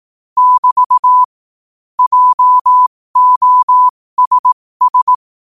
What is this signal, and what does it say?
Morse code, heard as a keyed tone, about 18 wpm, spaced as for 15 wpm, 1000 Hz. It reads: <BT> JOSS
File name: qotdfiles / 15wpm